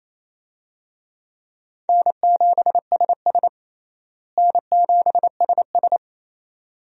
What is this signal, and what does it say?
Morse code, heard as a keyed tone, about 28 wpm, 700 Hz. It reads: N7SS N7SS
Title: USA Call Signs - 4 Characters - Repeated 2x - 28wpm